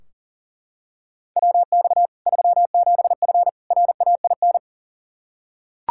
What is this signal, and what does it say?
Morse code, heard as a keyed tone, about 40 wpm, 700 Hz. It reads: WX 37F RAIN
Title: Practice QSO - Exchange 2 - Info - 40wpm